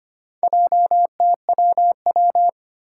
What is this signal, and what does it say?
Morse code, heard as a keyed tone, about 25 wpm, 700 Hz. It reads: JTWW
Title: Four Challenging Letter-Number Combinations 25wpm